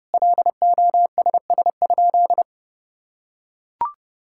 Morse code, heard as a keyed tone, about 30 wpm, 700 Hz. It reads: LOSS?
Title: Sentences from Top 1600 Words 30wpm